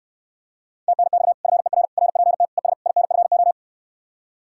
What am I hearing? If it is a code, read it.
Morse code, heard as a keyed tone, about 69 wpm, 700 Hz. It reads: TNX FER RPT ES INFO